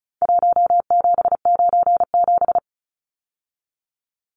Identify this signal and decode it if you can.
Morse code, heard as a keyed tone, about 34 wpm, 700 Hz. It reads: 1797
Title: Numbers - Contest Style - 4-Digit 35wpm